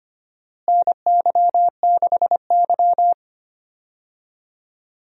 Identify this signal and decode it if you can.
Morse code, heard as a keyed tone, about 25 wpm, 700 Hz. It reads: NY6Y
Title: USA Call Signs - 4 Characters - Repeated 3x - 25wpm